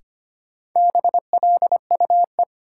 Morse code, heard as a keyed tone, about 25 wpm, 700 Hz. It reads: BLUE
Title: Sentences from Top 1200 Words - New Word Drill Alt - 25wpm